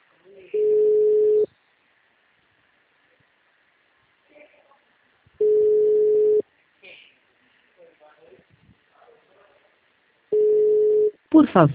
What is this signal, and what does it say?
Ringback tone (Europe, 425 Hz), with speech.
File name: out- -2017- - - .16477